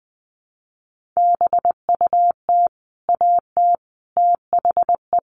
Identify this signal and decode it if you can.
Morse code, heard as a keyed tone, about 20 wpm, 700 Hz. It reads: BUT AT THE
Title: Sets of 5 Words Encore2 20wpm